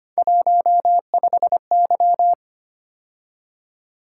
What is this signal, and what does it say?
Morse code, heard as a keyed tone, about 25 wpm, 700 Hz. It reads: 15Y
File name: Three Letter-Number Combinations - 25wpm